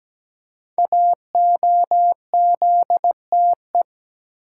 Morse code, heard as a keyed tone, about 17 wpm, 700 Hz. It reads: AOZTE
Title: Code Groups - 5 Letters - 17wpm